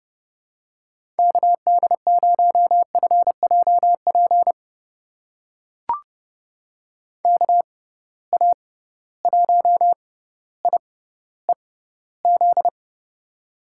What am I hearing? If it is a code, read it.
Morse code, heard as a keyed tone, about 32 wpm, spaced as for 16 wpm, 700 Hz. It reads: KD0FJP KA1IEZ
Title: USA Call Signs - 6 Characters - 6x Character Spacing - 30wpm